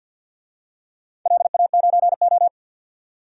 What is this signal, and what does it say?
Morse code, heard as a keyed tone, about 52 wpm, 700 Hz. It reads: LA9O
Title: DXCC-100 Call Signs - 4 Characters - 50wpm